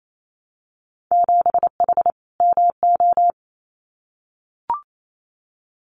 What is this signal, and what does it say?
Morse code, heard as a keyed tone, about 28 wpm, 700 Hz. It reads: 7H MO
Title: Winter Field Day Exchanges Encore 28wpm